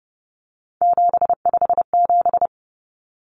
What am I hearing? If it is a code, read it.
Morse code, heard as a keyed tone, about 30 wpm, 700 Hz. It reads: 757
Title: Common Cut Numbers - 3-Digit 30wpm